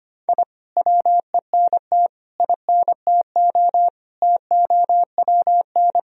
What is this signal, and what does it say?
Morse code, heard as a keyed tone, about 25 wpm, 700 Hz. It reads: I WENT INTO TOWN
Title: Sentences from Top 1000 Words - Review Part 4 - 25wpm